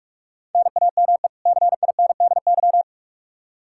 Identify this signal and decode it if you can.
Morse code, heard as a keyed tone, about 41 wpm, 700 Hz. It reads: NAME CINDY
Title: K1USN SST QSO Practice 45wpm